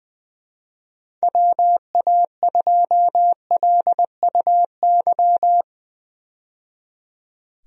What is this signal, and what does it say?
Morse code, heard as a keyed tone, about 21 wpm, 700 Hz. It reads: WA2LUY
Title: USA Call Signs - 6 Characters - 20wpm